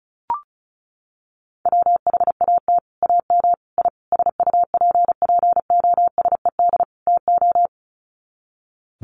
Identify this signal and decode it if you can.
Morse code, heard as a keyed tone, about 34 wpm, 700 Hz. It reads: WHAT AM I SUPPOSED TO
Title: Sets of 5 Words Encore2 35wpm